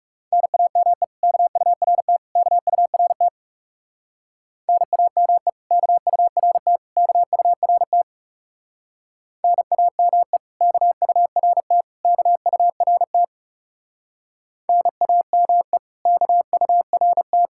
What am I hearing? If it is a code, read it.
Morse code, heard as a keyed tone, about 38 wpm, 700 Hz. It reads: NAME KURT KURT NAME KURT KURT NAME KURT KURT NAME KURT